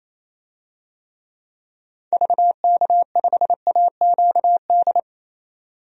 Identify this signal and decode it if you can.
Morse code, heard as a keyed tone, about 28 wpm, 700 Hz. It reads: VK5AQD
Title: DXCC-100 Call Signs - 6 Characters - 28wpm